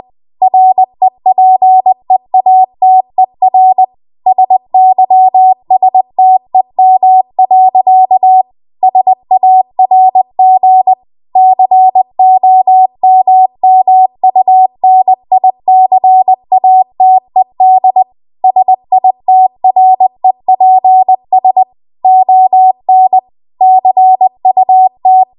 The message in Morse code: REPEATER SYSTEM. SARG COMMUNICATED SITREPS ON CUT

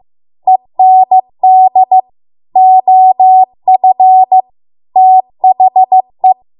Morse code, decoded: END OF THE

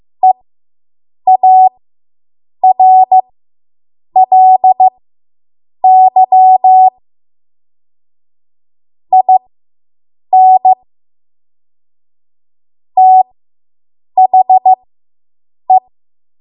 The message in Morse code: EARLY IN THE

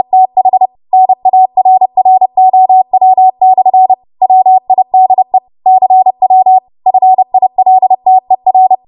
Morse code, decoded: TH NARROW/ WIDE CW FILTER

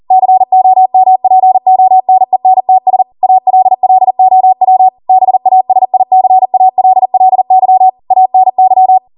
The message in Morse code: COMPONENTS ALLOW BASICALLY ANY

